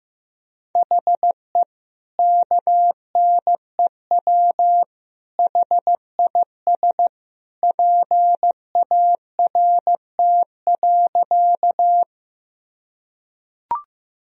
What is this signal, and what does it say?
Morse code, heard as a keyed tone, about 15 wpm, 700 Hz. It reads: HE KNEW HIS PART.
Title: Sentences from Top 1000 Words - Review Part 1 - 15wpm